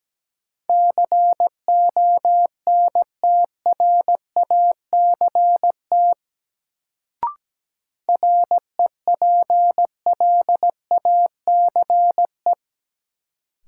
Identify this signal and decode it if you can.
Morse code, heard as a keyed tone, about 17 wpm, 700 Hz. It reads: CONTRACT REPLACE
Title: Sentences from Top 1400 Words - New Word Drill - 17wpm